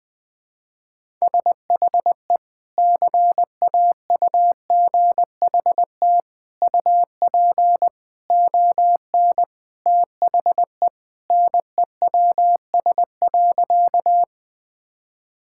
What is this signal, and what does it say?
Morse code, heard as a keyed tone, about 20 wpm, 700 Hz. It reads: SHE CAUGHT UP ON THE NEWS.
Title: Sentences with 3-Word Phrases - Spoken Beginning - 20wpm